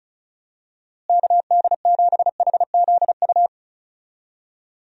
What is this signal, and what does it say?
Morse code, heard as a keyed tone, about 36 wpm, 700 Hz. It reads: KD7HZU
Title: USA Call Signs - 6 Characters - Repeated 3x - 35wpm